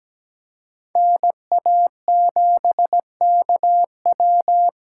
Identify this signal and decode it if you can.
Morse code, heard as a keyed tone, about 20 wpm, 700 Hz. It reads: NA7KW
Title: Speed Racing - USA 2x2 Call Signs 17wpm